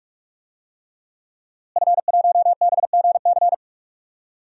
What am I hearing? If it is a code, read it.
Morse code, heard as a keyed tone, about 46 wpm, 700 Hz. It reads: F1BGC
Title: DXCC-100 Call Signs - 5 Characters - Repeated 3x - 45wpm